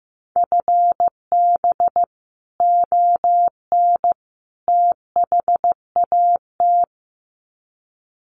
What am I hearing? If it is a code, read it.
Morse code, heard as a keyed tone, about 15 wpm, 700 Hz. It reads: FB ON THAT
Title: Common QSO Phrases 15wpm